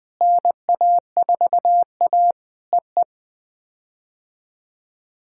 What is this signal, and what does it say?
Morse code, heard as a keyed tone, about 20 wpm, 700 Hz. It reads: NA4A EE
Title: POTA QSO Practice - 20wpm